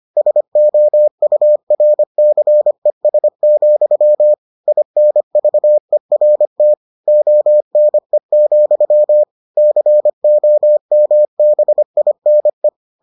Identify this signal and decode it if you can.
Morse code, heard as a keyed tone, about 25 wpm, 600 Hz. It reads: SOURCES, INVERT ONE, COMBINE